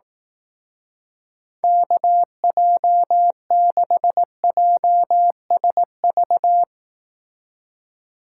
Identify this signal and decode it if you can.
Morse code, heard as a keyed tone, about 18 wpm, 700 Hz. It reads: KJ6JSV
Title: Speed Racing - USA 2x3 Call Signs 15wpm